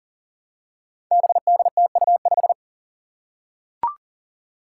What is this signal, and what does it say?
Morse code, heard as a keyed tone, about 40 wpm, 700 Hz. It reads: BDTU5
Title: Five Challenging Letter-Number Combinations 40wpm